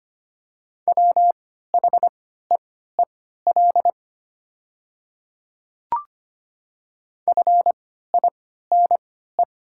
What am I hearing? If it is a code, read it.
Morse code, heard as a keyed tone, about 25 wpm, spaced as for 14 wpm, 700 Hz. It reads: WHEEL FINE
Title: 3x Character Spacing - Top 500 Words - 25wpm